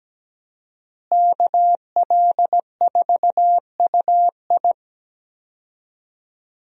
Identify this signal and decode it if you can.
Morse code, heard as a keyed tone, about 17 wpm, 700 Hz. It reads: KL4UI
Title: USA Call Signs - 5 Characters - Repeated 3x - 17wpm